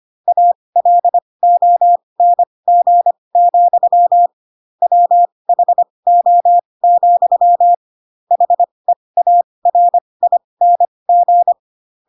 Morse code, heard as a keyed tone, about 25 wpm, spaced as for 20 wpm, 700 Hz. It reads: ALONG, WHO, HEARING